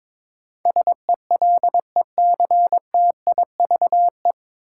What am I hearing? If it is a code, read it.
Morse code, heard as a keyed tone, about 22 wpm, 700 Hz. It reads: SELECTIVE
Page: Morse Code Ninja practice 22wpm